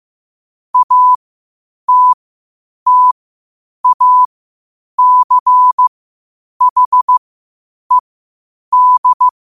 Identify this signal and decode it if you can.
Morse code, heard as a keyed tone, about 15 wpm, spaced as for 8 wpm, 1000 Hz. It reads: ATTACHED
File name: qotdfiles / 8wpm